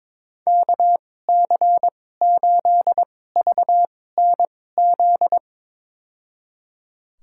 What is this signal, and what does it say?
Morse code, heard as a keyed tone, about 22 wpm, spaced as for 16 wpm, 700 Hz. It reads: KC8VNZ
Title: USA Call Signs - 6 Characters - 2x Character Spacing - 22wpm